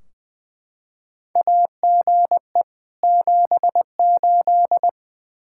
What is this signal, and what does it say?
Morse code, heard as a keyed tone, about 21 wpm, 700 Hz. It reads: AGE 78
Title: Practice QSO - Exchange 3 - More Info - 20wpm